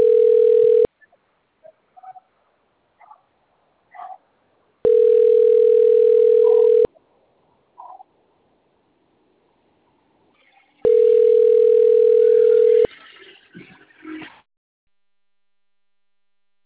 Ringback tone (North America), 440 + 480 Hz.
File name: rg-1600-unknown- - - .87491